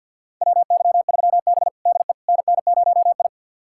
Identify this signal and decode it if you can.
Morse code, heard as a keyed tone, about 52 wpm, 700 Hz. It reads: WY3B DE NN0I
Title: USA Callsign DE Callsign - 4 Characters - 50wpm